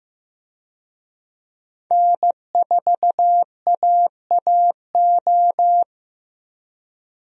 Morse code, heard as a keyed tone, about 15 wpm, 700 Hz. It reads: N4AAO
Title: USA Call Signs - 5 Characters - 15wpm